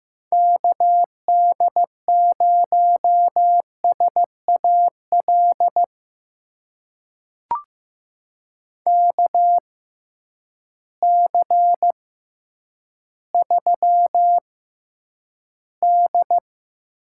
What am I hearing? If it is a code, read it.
Morse code, heard as a keyed tone, about 15 wpm, spaced as for 9 wpm, 700 Hz. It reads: KD0SAL KC3D
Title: USA Call Signs - 6 Characters - 6x Character Spacing - 15wpm